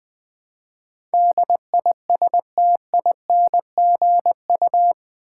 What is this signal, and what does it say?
Morse code, heard as a keyed tone, about 20 wpm, 700 Hz. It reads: DISTINGU
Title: Build A Word - 4 Letters At A Time - 11-Letter Words - 20wpm